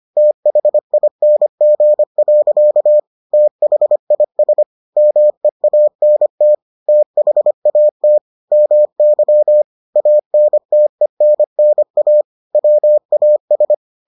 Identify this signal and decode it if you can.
Morse code, heard as a keyed tone, about 25 wpm, 600 Hz. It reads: THING. THIS MEANT THAT MY ANTENNA WAS